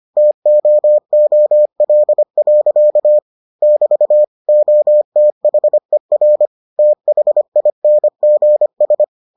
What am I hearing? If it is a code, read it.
Morse code, heard as a keyed tone, about 25 wpm, 600 Hz. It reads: TOOL. <BT> OTHER THINGS